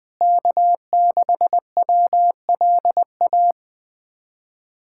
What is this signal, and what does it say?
Morse code, heard as a keyed tone, about 20 wpm, 700 Hz. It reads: K6WLA
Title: USA Callsign DE Callsign - 5 Characters - 20wpm